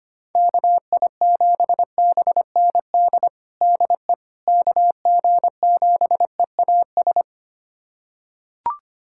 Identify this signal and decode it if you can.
Morse code, heard as a keyed tone, about 25 wpm, 700 Hz. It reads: KI7BND DE KG7EAS
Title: USA Callsign DE Callsign - 6 Characters - 25wpm